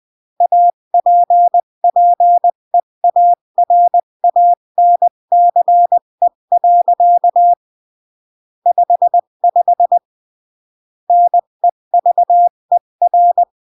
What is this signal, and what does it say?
Morse code, heard as a keyed tone, about 20 wpm, 700 Hz. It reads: APPEARANCE. 55 NEVER
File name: CW / jodorowsky / 82 commandments 20-15